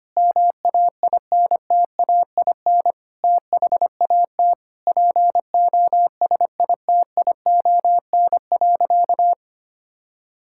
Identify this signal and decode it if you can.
Morse code, heard as a keyed tone, about 25 wpm, 700 Hz. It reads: MAINTAIN THAT POSITION.
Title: Sentences from Top 1000 Words - Review Part 5 - 25wpm